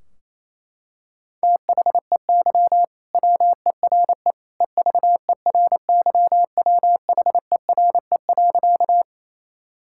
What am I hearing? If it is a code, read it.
Morse code, heard as a keyed tone, about 28 wpm, 700 Hz. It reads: THEY WERE EVERYWHERE.
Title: Sentences from Top 1600 Words 28wpm